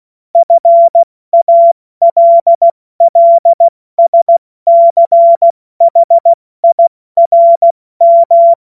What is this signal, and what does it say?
Morse code, heard as a keyed tone, about 16 wpm, 680 Hz. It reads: FALLSCHIRM
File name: Morsekurs / Witze / Witz 07. Dezember 2021, 70 F80